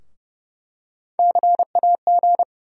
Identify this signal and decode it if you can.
Morse code, heard as a keyed tone, about 30 wpm, 700 Hz. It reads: CAG
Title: USA Call Sign Postfixes - 3 Letters 30wpm